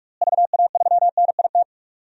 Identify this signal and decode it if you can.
Morse code, heard as a keyed tone, about 45 wpm, 700 Hz. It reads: VA3NIT
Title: DXCC-100 Call Signs - 6 Characters - Repeated 2x - 45wpm